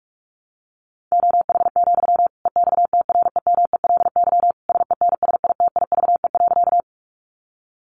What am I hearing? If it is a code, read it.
Morse code, heard as a keyed tone, about 44 wpm, 700 Hz. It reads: OH, EXTREMELY SENSITIVE.
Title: Sentences from Top 1400 Words 45wpm